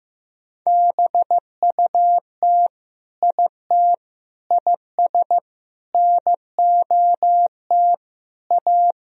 Morse code, heard as a keyed tone, about 15 wpm, 700 Hz. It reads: BUT IT IS NOT A
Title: Sentences from Top 1000 Words - Review Part 4 - 15wpm